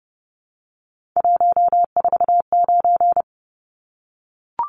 Morse code, heard as a keyed tone, about 30 wpm, 700 Hz. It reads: 149
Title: Numbers - 3-Digit 30wpm